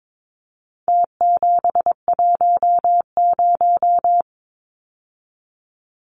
Morse code, heard as a keyed tone, about 22 wpm, 700 Hz. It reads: T710
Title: Numbers - Contest Style - 4-Digit 22wpm